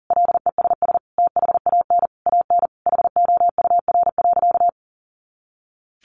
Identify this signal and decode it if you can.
Morse code, heard as a keyed tone, about 40 wpm, 700 Hz. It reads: LESS THAN AN HOUR.
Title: Sentences from Top 500 Words 40wpm